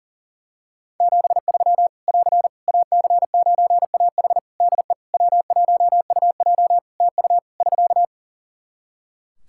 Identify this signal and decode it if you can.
Morse code, heard as a keyed tone, about 40 wpm, 700 Hz. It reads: 73 <AR> AC9AH DE W1UJ TU <SK>
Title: K1USN SST QSO Practice 40wpm